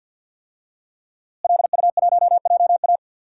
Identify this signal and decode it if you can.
Morse code, heard as a keyed tone, about 48 wpm, 700 Hz. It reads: LU1JA
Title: DXCC-100 Call Signs - 5 Characters - Repeated 3x - 50wpm